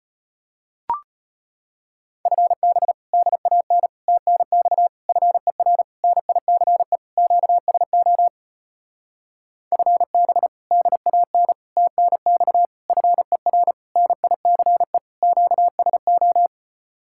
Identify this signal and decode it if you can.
Morse code, heard as a keyed tone, about 33 wpm, 700 Hz. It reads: FB DAN TNX FER NICE QSO FB DAN TNX FER NICE QSO